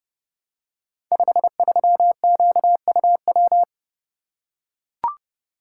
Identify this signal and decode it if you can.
Morse code, heard as a keyed tone, about 30 wpm, 700 Hz. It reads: 53QUW
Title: Code Groups - 5 Letter:Numbers - 30wpm